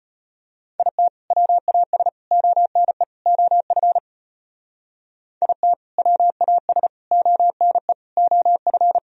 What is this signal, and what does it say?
Morse code, heard as a keyed tone, about 29 wpm, 700 Hz. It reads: IT WAS ONE OF IT WAS ONE OF
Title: Speed-Racing - Sets of 4 Words 25wpm